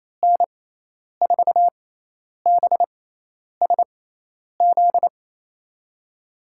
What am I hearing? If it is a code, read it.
Morse code, heard as a keyed tone, about 28 wpm, 700 Hz. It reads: N 4 B S Z
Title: USA Call Signs - 6x Character Spacing - 28wpm